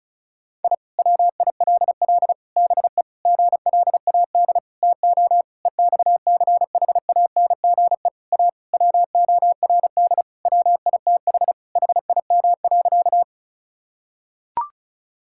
Morse code, heard as a keyed tone, about 36 wpm, 700 Hz. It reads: I WILL BE GLAD TO EXCHANGE A WORD WITH HIM.
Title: Sentences from Top 1000 Words - Review Part 6 - 35wpm